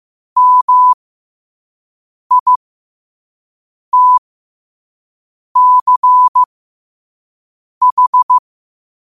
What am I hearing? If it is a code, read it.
Morse code, heard as a keyed tone, about 15 wpm, spaced as for 5 wpm, 1000 Hz. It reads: MITCH
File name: qotdfiles / 5wpm